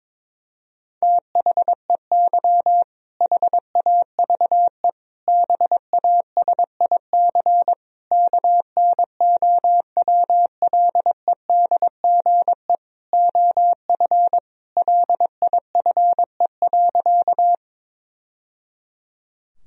Morse code, heard as a keyed tone, about 22 wpm, 700 Hz. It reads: THEY HAVE BASIC KNOWLEDGE OF LIFE.
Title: Sentences from Top 1000 Words - Review Part 3 - 22wpm